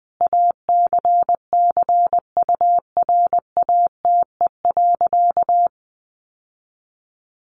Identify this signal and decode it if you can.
Morse code, heard as a keyed tone, about 20 wpm, 700 Hz. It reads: ACCURATE.
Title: Sentences from Top 1200 Words 20wpm